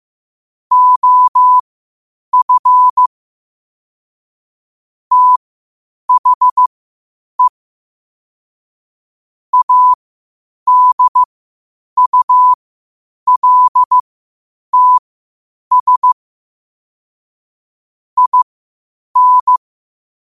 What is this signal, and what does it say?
Morse code, heard as a keyed tone, about 15 wpm, spaced as for 8 wpm, 1000 Hz. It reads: OF THE ADULTS IN
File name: qotdfiles / 8wpm